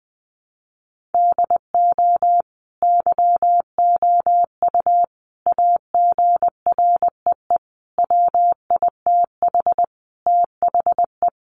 Morse code, heard as a keyed tone, about 20 wpm, 700 Hz. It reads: DO YOU AGREE WITH THE
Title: Sentences from Top 1400 Words 20wpm